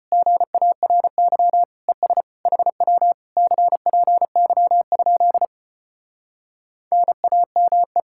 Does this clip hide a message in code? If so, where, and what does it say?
Yes: Morse code from 0.1 to 8.0 s, heard as a keyed tone, about 33 wpm, 700 Hz. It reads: GARY ES HW CPY? NAME